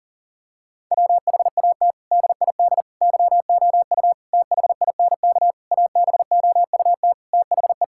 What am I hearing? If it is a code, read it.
Morse code, heard as a keyed tone, about 40 wpm, 700 Hz. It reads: WHAT DID YOU THINK ABOUT THE
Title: Sentences from Top 1000 Words - Review Part 5 - 40wpm